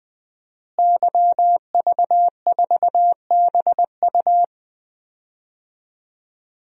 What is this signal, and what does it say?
Morse code, heard as a keyed tone, about 21 wpm, 700 Hz. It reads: YV4BU
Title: DXCC-100 Call Signs - 5 Characters - Repeated 3x - 20wpm